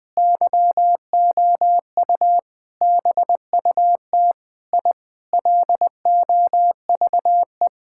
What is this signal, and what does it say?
Morse code, heard as a keyed tone, about 20 wpm, 700 Hz. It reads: YOU BUT I LOVE
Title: Sentences with 7-Word Phrases - 20wpm